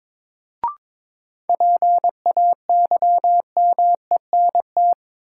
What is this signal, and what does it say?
Morse code, heard as a keyed tone, about 22 wpm, 700 Hz. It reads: PAYMENT
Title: Sentences from Top 1200 Words - New Word Drill - 22wpm